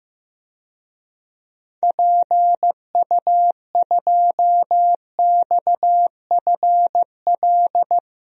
Morse code, heard as a keyed tone, about 15 wpm, 700 Hz. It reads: PU2XFL
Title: DXCC-100 Call Signs - 6 Characters - Repeated 2x - 15wpm